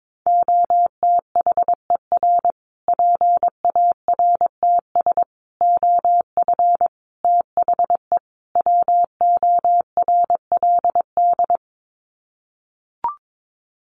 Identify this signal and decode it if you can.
Morse code, heard as a keyed tone, about 22 wpm, 700 Hz. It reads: OTHER PARTS OF THE WORLD
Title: Sets of 5 Words Encore 22wpm